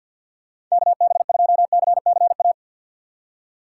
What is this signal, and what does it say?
Morse code, heard as a keyed tone, about 53 wpm, 700 Hz. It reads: KB2CCA
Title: USA Call Signs - 6 Characters - 8x Character Spacing - 50wpm